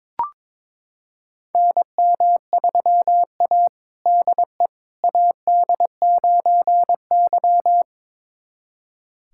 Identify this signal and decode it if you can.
Morse code, heard as a keyed tone, about 22 wpm, 700 Hz. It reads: NM3A DE AD9Y
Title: USA Callsign DE Callsign - 4 Characters - 22wpm